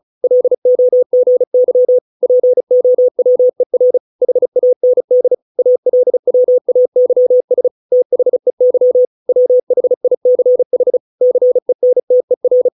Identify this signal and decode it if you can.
Morse code, heard as a keyed tone, about 34 wpm, 500 Hz. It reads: LOGY POWER HAND ALWAYS THEY WHICH CENTER